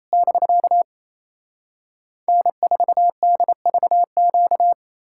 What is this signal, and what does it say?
Morse code, heard as a keyed tone, about 38 wpm, spaced as for 33 wpm, 700 Hz. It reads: <BK> N4DVQ
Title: Speed-Racing Common QSO Phrases - 28wpm